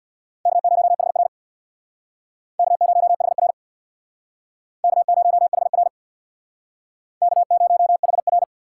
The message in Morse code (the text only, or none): K0HR K0HR K0HR K0HR